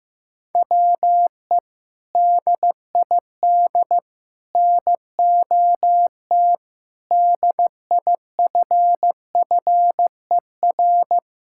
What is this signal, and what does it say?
Morse code, heard as a keyed tone, about 15 wpm, 700 Hz. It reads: WE DID NOT DIFFER
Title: Sentences from Top 1000 Words - Review Part 6 - 15wpm